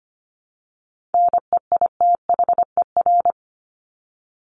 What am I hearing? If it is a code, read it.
Morse code, heard as a keyed tone, about 25 wpm, 700 Hz. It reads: NEITHER
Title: Top 1501-1750 Words 25wpm